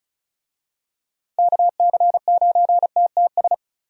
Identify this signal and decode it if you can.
Morse code, heard as a keyed tone, about 36 wpm, 700 Hz. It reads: KC9TTS
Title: USA Call Signs - 6 Characters - Repeated 3x - 35wpm